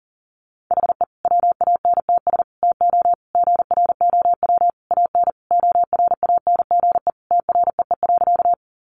Morse code, heard as a keyed tone, about 40 wpm, 700 Hz. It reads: HE WANTS TO GROW AN ORANGE TREE.
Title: Easy Phrases 40wpm